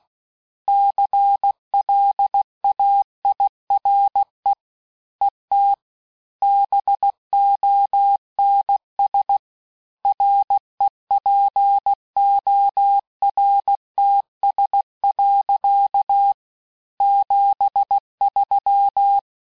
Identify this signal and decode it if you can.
Morse code, heard as a keyed tone, about 16 wpm, 800 Hz. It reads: CLAIRE ET BONS REPORTS. 73
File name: cours-du-lundi files / cours-1580-fin-QSO